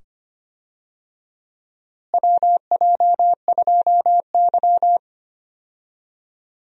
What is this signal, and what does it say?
Morse code, heard as a keyed tone, about 25 wpm, 700 Hz. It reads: WJ2Y
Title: USA Call Signs - 4 Characters - Repeated 2x - 25wpm